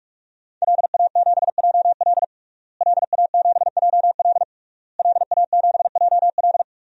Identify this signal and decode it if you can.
Morse code, heard as a keyed tone, about 41 wpm, 700 Hz. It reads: LA7JL LA7JL LA7JL
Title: DXCC-100 Call Signs - 5 Characters - Repeated 3x - 45wpm